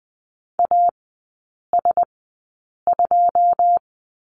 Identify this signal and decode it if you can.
Morse code, heard as a keyed tone, about 20 wpm, spaced as for 8 wpm, 700 Hz. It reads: AS2
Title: 114 Lesson 19e - Letter w - Three Characters 20wpm